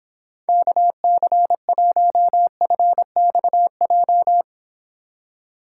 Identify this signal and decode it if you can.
Morse code, heard as a keyed tone, about 26 wpm, 700 Hz. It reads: KC1FXJ